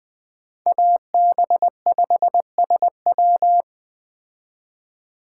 Morse code, heard as a keyed tone, about 21 wpm, 700 Hz. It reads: AB5SW